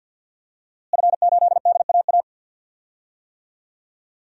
Morse code, heard as a keyed tone, about 46 wpm, 700 Hz. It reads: F8DAA
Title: DXCC-100 Call Signs - 5 Characters - Repeated 3x - 50wpm